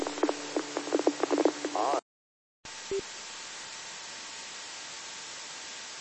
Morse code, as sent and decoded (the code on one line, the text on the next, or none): none